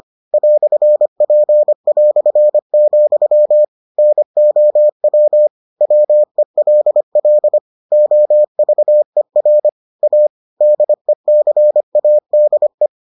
Morse code, heard as a keyed tone, about 25 wpm, 600 Hz. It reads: "P", NOW WELL OVER A DECADE